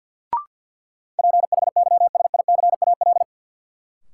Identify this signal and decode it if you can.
Morse code, heard as a keyed tone, about 52 wpm, 700 Hz. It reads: PHYSICAL
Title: Sentences from Top 1400 Words - New Word Drill Alt - 50wpm